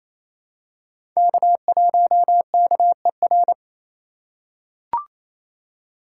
Morse code, heard as a keyed tone, about 28 wpm, 700 Hz. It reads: K1KER
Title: USA Call Signs - 4x Character Spacing - 28wpm